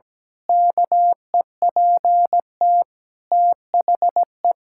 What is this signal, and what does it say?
Morse code, heard as a keyed tone, about 17 wpm, 700 Hz. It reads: KEPT THE
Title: Pangram Challenge v2 - 17wpm